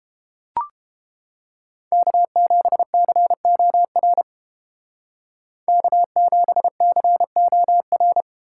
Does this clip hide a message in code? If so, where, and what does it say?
Yes: Morse code from 1.9 to 8.2 s, heard as a keyed tone, about 31 wpm, 700 Hz. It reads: K7COR K7COR